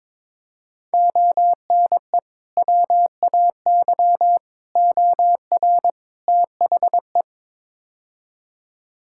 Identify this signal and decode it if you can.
Morse code, heard as a keyed tone, about 22 wpm, 700 Hz. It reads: ONE WAY OR THE
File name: practice / 22wpm / Build A Phrase - 5-Word Phrases - 22wpm